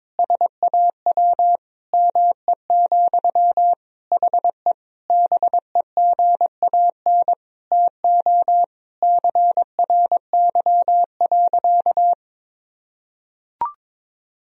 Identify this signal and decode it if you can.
Morse code, heard as a keyed tone, about 22 wpm, 700 Hz. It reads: SAW ME, HE BEGAN TO CRY.
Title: Sentences from Top 1000 Words - Review Part 2 - 22wpm